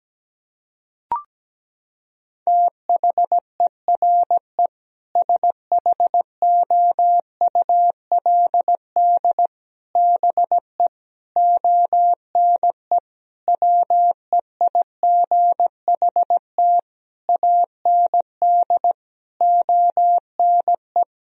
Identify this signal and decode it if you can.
Morse code, heard as a keyed tone, about 17 wpm, 700 Hz. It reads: THERE SHOULD BE ONE WEIGHT AND ONE
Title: Sentences from Top 1000 Words - Review Part 3 - 17wpm